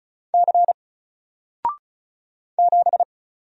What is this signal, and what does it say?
Morse code, heard as a keyed tone, about 36 wpm, 700 Hz. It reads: C 7
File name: Single-Letter-Number-35wpm